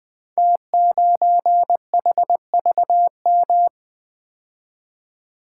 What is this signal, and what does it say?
Morse code, heard as a keyed tone, about 20 wpm, 700 Hz. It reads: T9HVM
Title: Code Groups - 5 Letter:Numbers - 20wpm